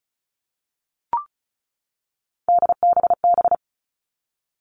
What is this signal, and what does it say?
Morse code, heard as a keyed tone, about 34 wpm, 700 Hz. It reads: DBB
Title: All Cut Numbers - 3-Digit 35wpm